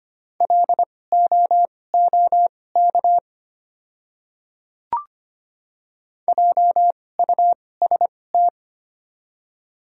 Morse code, heard as a keyed tone, about 25 wpm, spaced as for 18 wpm, 700 Hz. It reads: LOOK JUST
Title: 2x Character Spacing - Top 300 Words - 25wpm